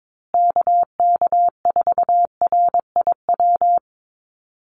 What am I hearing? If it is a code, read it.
Morse code, heard as a keyed tone, about 22 wpm, 700 Hz. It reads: KK4RIW
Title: USA Call Signs Repeated 3x 22wpm